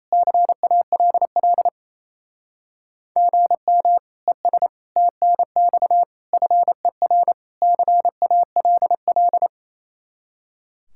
Morse code, heard as a keyed tone, about 36 wpm, 700 Hz. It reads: CALL GM ES TNX FER CALL